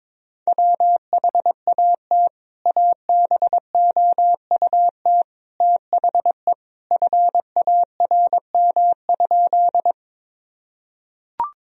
Morse code, heard as a keyed tone, about 22 wpm, 700 Hz. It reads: WHAT ABOUT THE FARM?
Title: Sentences from Top 1000 Words - Review Part 2 - 22wpm